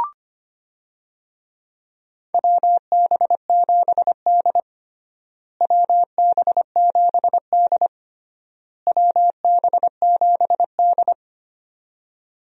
Morse code, heard as a keyed tone, about 25 wpm, 700 Hz. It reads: WB7D WB7D WB7D
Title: USA Call Signs - 4 Characters - Repeated 3x - 25wpm